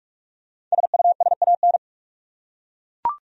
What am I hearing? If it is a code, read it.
Morse code, heard as a keyed tone, about 41 wpm, 700 Hz. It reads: SUSAN
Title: USA Common Names - Max 5 Letters - 45wpm